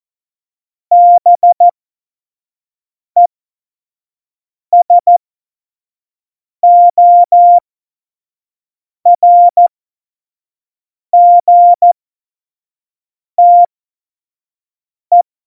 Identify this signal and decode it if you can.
Morse code, heard as a keyed tone, about 14 wpm, spaced as for 5 wpm, 700 Hz. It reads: BESORGTE
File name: Morsekurs / Witze / Witz 17 April 2018, 25 F70